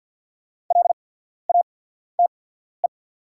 Morse code, heard as a keyed tone, about 48 wpm, spaced as for 14 wpm, 700 Hz. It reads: LATE
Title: 8x Character Spacing - Top 300 Words - 50wpm